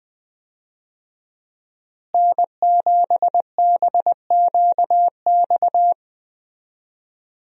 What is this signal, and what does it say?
Morse code, heard as a keyed tone, about 20 wpm, 700 Hz. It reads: N7BQX
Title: USA Call Signs - 5 Characters - Repeated 3x - 20wpm